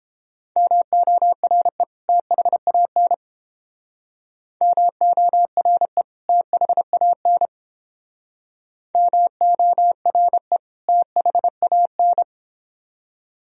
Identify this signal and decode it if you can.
Morse code, heard as a keyed tone, about 30 wpm, 700 Hz. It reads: MORE THAN MORE THAN MORE THAN